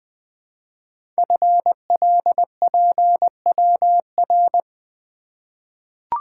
Morse code, heard as a keyed tone, about 20 wpm, 700 Hz. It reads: FLPWR
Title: Code Groups - 5 Letters - 20wpm